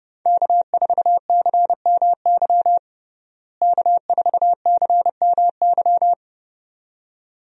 Morse code, heard as a keyed tone, about 31 wpm, 700 Hz. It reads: K4CMY K4CMY
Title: USA Call Signs - 5 Characters - Repeated 2x - 30wpm